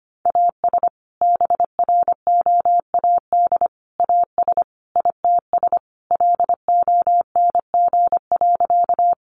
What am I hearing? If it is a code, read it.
Morse code, heard as a keyed tone, about 25 wpm, 700 Hz. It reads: AS BROAD AS ITS LONG.
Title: Sentences from Top 1000 Words 25wpm